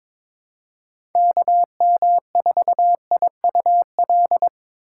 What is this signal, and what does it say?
Morse code, heard as a keyed tone, about 22 wpm, 700 Hz. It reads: KM4IUL
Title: USA Call Signs - 6 Characters - 4x Character Spacing - 22wpm